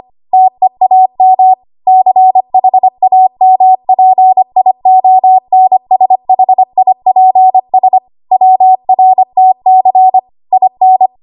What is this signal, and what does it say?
Morse code, heard as a keyed tone, about 25 wpm, 750 Hz. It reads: TEAM CHAMPIONSHIPS WRTC IN
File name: Archive / 25 WPM / 25WPM